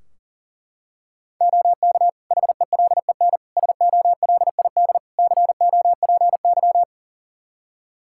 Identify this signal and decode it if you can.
Morse code, heard as a keyed tone, about 37 wpm, 700 Hz. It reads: OK HELEN SOLID COPY
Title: Practice QSO - Exchange 3 - More Info - 40wpm